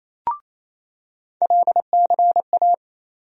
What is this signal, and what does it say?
Morse code, heard as a keyed tone, about 28 wpm, 700 Hz. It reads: LCA
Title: USA Call Sign Postfixes - 3 Letters 28wpm